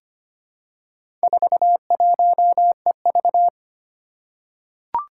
Morse code, heard as a keyed tone, about 25 wpm, 700 Hz. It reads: 41EV
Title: Four Challenging Letter-Number Combinations 25wpm